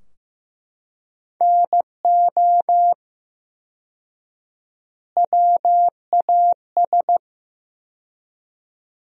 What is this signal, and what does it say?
Morse code, heard as a keyed tone, about 15 wpm, 700 Hz. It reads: NO WAS
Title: 4x Word-Spacing - Sentences from Top 100 Words - 15wpm